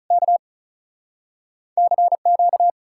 Morse code, heard as a keyed tone, about 46 wpm, 700 Hz. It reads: K CQ